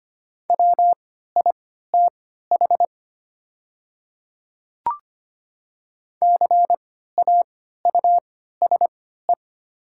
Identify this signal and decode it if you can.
Morse code, heard as a keyed tone, about 25 wpm, spaced as for 14 wpm, 700 Hz. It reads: WITH CAUSE
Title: 3x Character Spacing - Top 200 Words - 25wpm